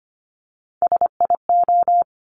Morse code, heard as a keyed tone, about 25 wpm, 700 Hz. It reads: SIO
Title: Top Three Letter Combinations in Words 25wpm